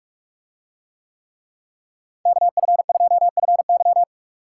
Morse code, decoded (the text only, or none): KF2FY